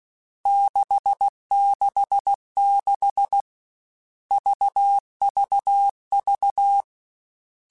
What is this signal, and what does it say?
Morse code, heard as a keyed tone, about 16 wpm, 800 Hz. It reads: 666 VVV
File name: Groups@13 / Groups@13 mp3s / VB46@13